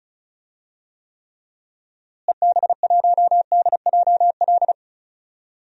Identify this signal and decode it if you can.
Morse code, heard as a keyed tone, about 36 wpm, 700 Hz. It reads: EB1DJL
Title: DXCC-100 Call Signs - 6 Characters - Repeated 2x - 35wpm